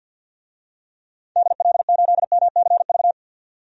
Morse code, heard as a keyed tone, about 53 wpm, 700 Hz. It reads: DL8MCV
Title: DXCC-100 Call Signs - 6 Characters - 50wpm